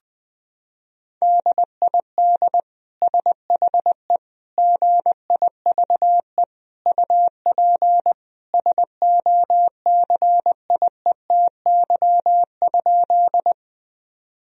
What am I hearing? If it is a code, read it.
Morse code, heard as a keyed tone, about 19 wpm, 700 Hz. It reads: DID SHE GIVE UP SOCIETY?
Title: Sentences from Top 1000 Words - Review Part 4 - 20wpm